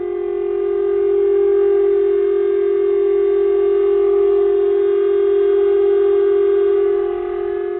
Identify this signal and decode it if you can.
Dial tone (Japan), 400 Hz.